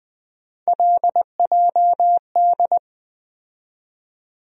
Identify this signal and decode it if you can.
Morse code, heard as a keyed tone, about 20 wpm, 700 Hz. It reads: LJD
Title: USA Call Sign Postfixes - 3 Letters 20wpm